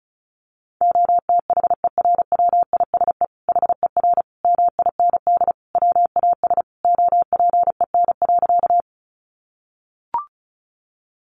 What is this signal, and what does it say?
Morse code, heard as a keyed tone, about 34 wpm, 700 Hz. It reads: OTHERWISE HER MIND WAS OPEN.
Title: Sentences from Top 1200 Words 35wpm